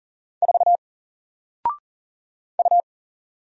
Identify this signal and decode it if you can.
Morse code, heard as a keyed tone, about 39 wpm, 700 Hz. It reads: 4 U
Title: Single Character (Letter:Number:Punctuation:Prosign) 40wpm